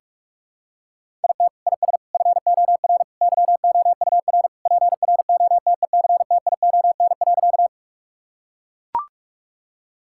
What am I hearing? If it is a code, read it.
Morse code, heard as a keyed tone, about 43 wpm, 700 Hz. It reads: IT IS FOR YOUR PROTECTION.
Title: Sentences from Top 1600 Words 45wpm